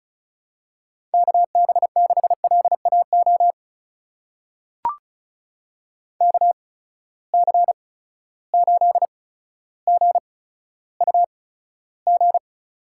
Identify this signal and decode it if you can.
Morse code, heard as a keyed tone, about 34 wpm, spaced as for 15 wpm, 700 Hz. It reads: KB6LAO KC8GUG
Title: USA Call Signs - 6 Characters - 8x Character Spacing - 35wpm